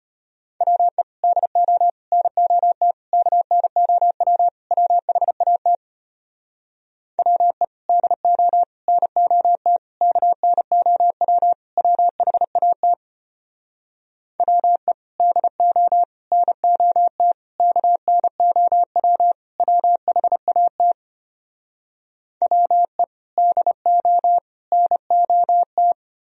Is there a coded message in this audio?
Yes: Morse code from 0.6 to 25.9 s, heard as a keyed tone, about 33 wpm, 700 Hz. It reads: WE DO NOT KNOW WHAT WE DO NOT KNOW WHAT WE DO NOT KNOW WHAT WE DO NOT